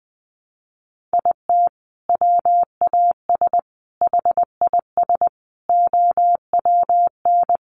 Morse code, heard as a keyed tone, about 20 wpm, 700 Hz. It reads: IT WAS HIS OWN
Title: Sentences from Top 300 Words 20wpm